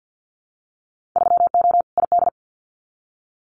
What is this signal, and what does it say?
Morse code, heard as a keyed tone, about 50 wpm, 700 Hz. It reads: 3O ID
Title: Winter Field Day Exchanges Encore 50wpm